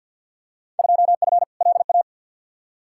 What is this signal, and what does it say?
Morse code, heard as a keyed tone, about 53 wpm, 700 Hz. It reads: 2F LA
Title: Field Day Exchanges - 50wpm